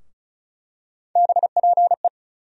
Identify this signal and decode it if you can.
Morse code, heard as a keyed tone, about 36 wpm, 700 Hz. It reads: BPE
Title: USA Call Sign Postfixes - 3 Letters 35wpm